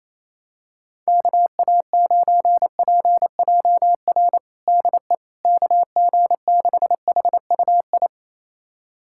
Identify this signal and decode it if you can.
Morse code, heard as a keyed tone, about 28 wpm, 700 Hz. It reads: KA9PJR DE KG6HUI
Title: USA Callsign DE Callsign 28wpm